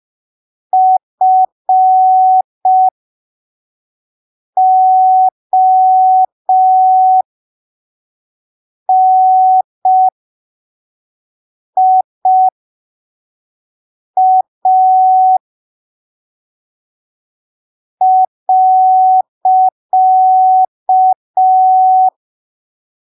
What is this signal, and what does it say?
Morse code, heard as a keyed tone, about 5 wpm, spaced as for 3 wpm, 750 Hz. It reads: FONIA .
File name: arquivos / texto / texto 04 5ppm